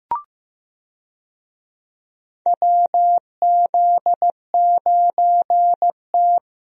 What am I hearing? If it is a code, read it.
Morse code, heard as a keyed tone, about 15 wpm, 700 Hz. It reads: WZ9T
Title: USA Call Signs - 4 Characters - Repeated 2x - 15wpm